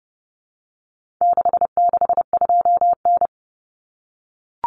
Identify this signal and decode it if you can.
Morse code, heard as a keyed tone, about 30 wpm, 700 Hz. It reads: 662N
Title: Common Cut Numbers - 4-Digit 30wpm